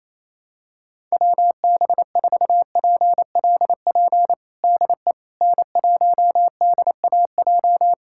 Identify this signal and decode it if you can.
Morse code, heard as a keyed tone, about 28 wpm, 700 Hz. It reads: WB4PLP DE N1DAJ
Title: USA Callsign DE Callsign 28wpm